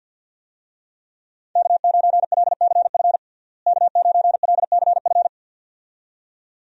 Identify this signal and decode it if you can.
Morse code, heard as a keyed tone, about 49 wpm, 700 Hz. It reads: K9LCF K9LCF
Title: USA Call Signs - 5 Characters - Repeated 2x - 50wpm